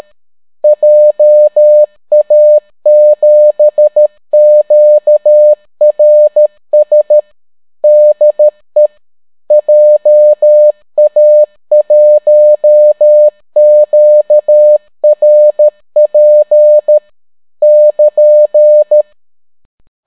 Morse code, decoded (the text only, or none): JA7QRS DE JA1QRP <KN>